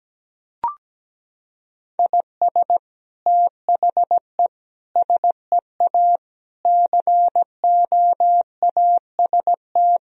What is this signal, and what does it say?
Morse code, heard as a keyed tone, about 17 wpm, 700 Hz. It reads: IS THE SEA COAST
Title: Sentences from Top 1000 Words - Review Part 4 - 17wpm